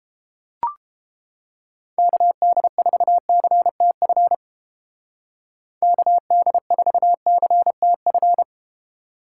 Morse code, heard as a keyed tone, about 31 wpm, 700 Hz. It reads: KD4CTF KD4CTF